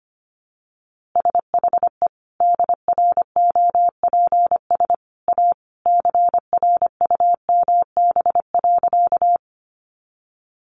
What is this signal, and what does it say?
Morse code, heard as a keyed tone, about 25 wpm, 700 Hz. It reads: SHE DROPS A CRUMB.
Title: Easy Phrases 25wpm